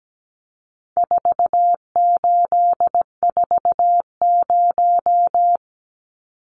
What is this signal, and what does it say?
Morse code, heard as a keyed tone, about 17 wpm, 700 Hz. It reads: 4840
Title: Numbers - 4-Digit 17wpm